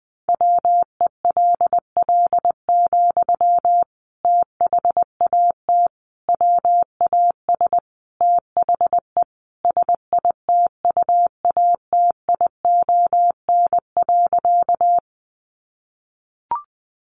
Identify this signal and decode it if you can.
Morse code, heard as a keyed tone, about 20 wpm, 700 Hz. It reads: WELL, THAT WAS THE SITUATION.
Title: Sentences from Top 1200 Words 20wpm